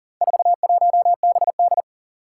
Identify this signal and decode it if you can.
Morse code, heard as a keyed tone, about 40 wpm, 700 Hz. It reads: 41BD
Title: Four Challenging Letter-Number Combinations 40wpm